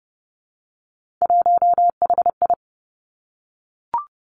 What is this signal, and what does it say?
Morse code, heard as a keyed tone, about 30 wpm, 700 Hz. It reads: 1HI
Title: Three Challenging Letter-Number Combinations 30wpm